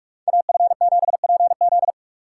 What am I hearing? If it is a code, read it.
Morse code, heard as a keyed tone, about 45 wpm, 700 Hz. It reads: AF7PZ